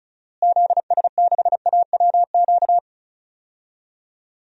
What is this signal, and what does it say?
Morse code, heard as a keyed tone, about 36 wpm, 700 Hz. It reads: ZS6AWQ
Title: DXCC-100 Call Signs - 6 Characters - Repeated 3x - 35wpm